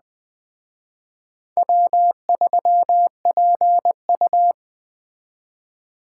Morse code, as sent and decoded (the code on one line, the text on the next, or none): .-- ...-- .--. ..-
W3PU